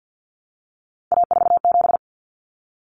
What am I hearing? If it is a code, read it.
Morse code, heard as a keyed tone, about 48 wpm, 700 Hz. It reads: A47
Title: Common Cut Numbers - 3-Digit 50wpm